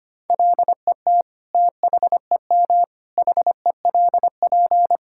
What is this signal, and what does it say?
Morse code, heard as a keyed tone, about 25 wpm, 700 Hz. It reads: LET THEM HELP
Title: Sentences from Top 1600 Words 25wpm